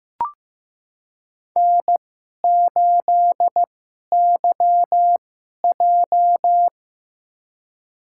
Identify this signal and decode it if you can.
Morse code, heard as a keyed tone, about 15 wpm, spaced as for 11 wpm, 700 Hz. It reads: N8YJ
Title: USA Call Signs - 4 Characters - 2x Character Spacing - 15wpm